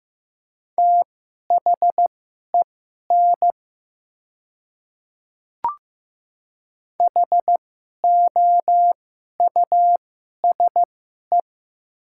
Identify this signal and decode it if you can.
Morse code, heard as a keyed tone, about 15 wpm, spaced as for 11 wpm, 700 Hz. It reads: THEN HOUSE
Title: 2x Character Spacing - Top 200 Words - 15wpm